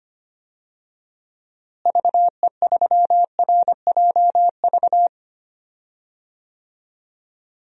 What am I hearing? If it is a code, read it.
Morse code, heard as a keyed tone, about 25 wpm, 700 Hz. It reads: VE3RJV
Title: DXCC-100 Call Signs - 6 Characters - Repeated 2x - 25wpm